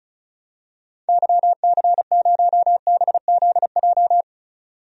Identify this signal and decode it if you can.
Morse code, heard as a keyed tone, about 36 wpm, 700 Hz. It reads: YC0BZJ
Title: DXCC-100 Call Signs - 6 Characters - 35wpm